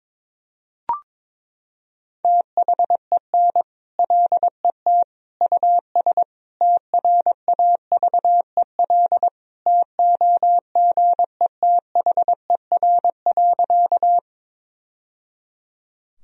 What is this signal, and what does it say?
Morse code, heard as a keyed tone, about 21 wpm, 700 Hz. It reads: THEN LET US TRAVEL TOGETHER.
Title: Sentences from Top 1000 Words - Review Part 2 - 22wpm